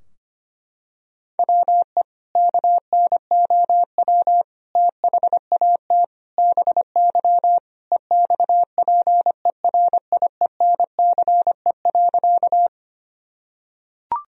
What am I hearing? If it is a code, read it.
Morse code, heard as a keyed tone, about 25 wpm, 700 Hz. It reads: WE KNOW THAT BY EXPERIENCE.
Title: Sentences from Top 1000 Words - Review Part 3 - 25wpm